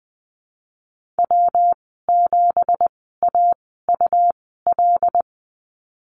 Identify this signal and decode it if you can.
Morse code, heard as a keyed tone, about 20 wpm, spaced as for 15 wpm, 700 Hz. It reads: W7AUL
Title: 179 Lesson 29g - Slash - Call Signs 20wpm